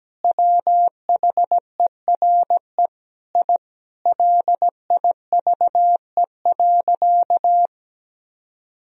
Morse code, heard as a keyed tone, about 17 wpm, 700 Hz. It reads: WHERE I LIVE.
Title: Sentences from Top 1000 Words - Review Part 3 - 17wpm